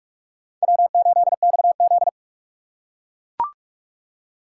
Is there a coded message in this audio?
Yes: Morse code from 0.6 to 2.1 s, heard as a keyed tone, about 42 wpm, 700 Hz. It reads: W8XZ